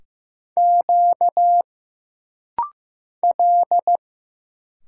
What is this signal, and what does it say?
Morse code, heard as a keyed tone, about 15 wpm, 700 Hz. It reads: Q L
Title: Single Letters - Rapid-Fire 15wpm